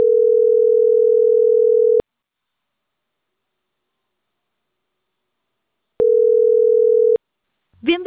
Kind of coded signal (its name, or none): telephone call progress tone